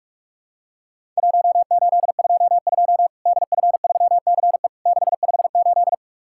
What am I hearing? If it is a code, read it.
Morse code, heard as a keyed tone, about 44 wpm, 700 Hz. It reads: 1822 DF3CE 658
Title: Worked All Europe DX Contest - QTC Exchange - 45wpm